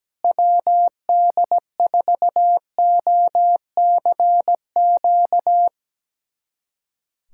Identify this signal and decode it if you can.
Morse code, heard as a keyed tone, about 17 wpm, 700 Hz. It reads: WD4OCQ
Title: USA Callsign DE Callsign 17wpm